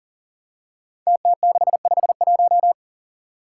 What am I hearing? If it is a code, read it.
Morse code, heard as a keyed tone, about 37 wpm, 700 Hz. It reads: TT651
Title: Five Challenging Letter-Number Combinations 40wpm